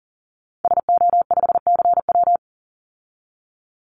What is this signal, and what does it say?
Morse code, heard as a keyed tone, about 40 wpm, 700 Hz. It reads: SO5CW
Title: Call Signs - 40wpm